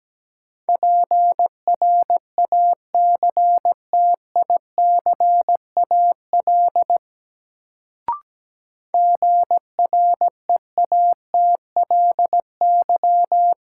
Morse code, heard as a keyed tone, about 17 wpm, 700 Hz. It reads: PRACTICAL GREATLY
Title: Sentences from Top 1400 Words - New Word Drill - 17wpm